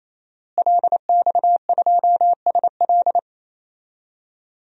Morse code, heard as a keyed tone, about 27 wpm, 700 Hz. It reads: LX2SL
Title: DXCC-100 Call Signs - 5 Characters - Repeated 3x - 28wpm